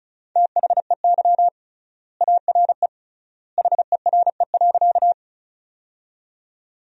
Morse code, heard as a keyed tone, about 36 wpm, 700 Hz. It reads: THEY ARE HERE.
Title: 3x Word-Spacing - Sentences from Top 200 Words - 35wpm